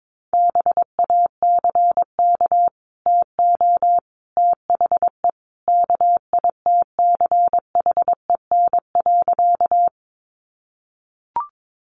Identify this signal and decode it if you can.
Morse code, heard as a keyed tone, about 22 wpm, 700 Hz. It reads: BACK TO THE KITCHEN.
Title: Sentences from Top 1400 Words 22wpm